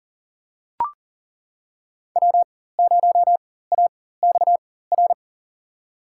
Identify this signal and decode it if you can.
Morse code, heard as a keyed tone, about 38 wpm, spaced as for 18 wpm, 700 Hz. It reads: W0AXR
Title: USA Call Signs - 5 Characters - 4x Character Spacing - 40wpm